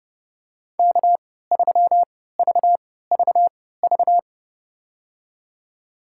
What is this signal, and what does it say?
Morse code, heard as a keyed tone, about 30 wpm, spaced as for 17 wpm, 700 Hz. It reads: K3VVV
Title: USA Call Signs - 5 Characters - 3x Character Spacing - 30wpm